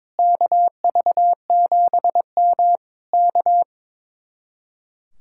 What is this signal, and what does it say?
Morse code, heard as a keyed tone, about 22 wpm, 700 Hz. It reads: KV7M K
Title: Speed-Racing Common QSO Phrases - 22wpm